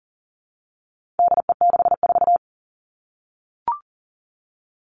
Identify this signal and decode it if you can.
Morse code, heard as a keyed tone, about 40 wpm, 700 Hz. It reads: DE64
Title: All Cut Numbers - 4-Digit 40wpm